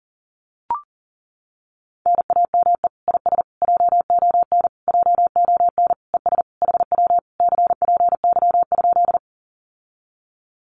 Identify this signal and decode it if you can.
Morse code, heard as a keyed tone, about 40 wpm, 700 Hz. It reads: NAME IS JON JON ES HW CPY?
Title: Common QSO Phrases Encore 40wpm